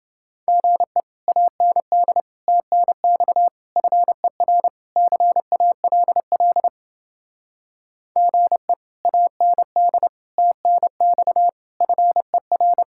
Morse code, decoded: GE AND TNX FER CALL GE AND TNX FER

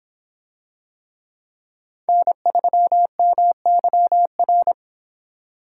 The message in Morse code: N3MYR

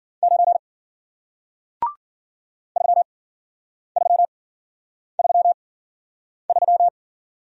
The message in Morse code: 9 3333

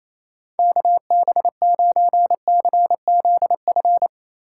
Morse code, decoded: KB9CZF

